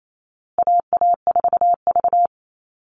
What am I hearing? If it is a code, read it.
Morse code, heard as a keyed tone, about 28 wpm, 700 Hz. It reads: AA4V
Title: Call Signs Repeated 2x - 28wpm